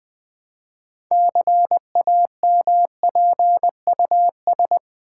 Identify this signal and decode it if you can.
Morse code, heard as a keyed tone, about 20 wpm, 700 Hz. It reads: CAMPUS
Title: Build A Word - 3 Letters At A Time - 6-Letter Words - 20wpm